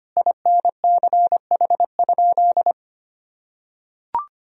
Morse code, heard as a keyed tone, about 25 wpm, 700 Hz. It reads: INCH?
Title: Sentences from Top 1000 Words - Review Part 3 - 25wpm